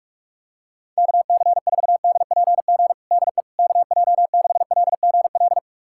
Morse code, heard as a keyed tone, about 46 wpm, 700 Hz. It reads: KC4DPG DE KJ6LGL
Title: USA Callsign DE Callsign - 6 Characters - 45wpm